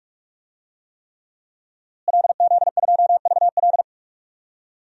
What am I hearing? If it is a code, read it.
Morse code, heard as a keyed tone, about 41 wpm, 700 Hz. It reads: LZ2VL
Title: DXCC-100 Call Signs - 5 Characters - Repeated 3x - 45wpm